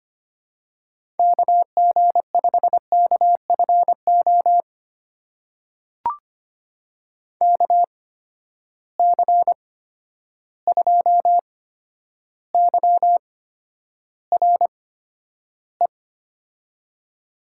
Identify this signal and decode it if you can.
Morse code, heard as a keyed tone, about 25 wpm, spaced as for 11 wpm, 700 Hz. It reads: KG5KFO KC2YRE
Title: USA Call Signs - 6 Characters - 8x Character Spacing - 25wpm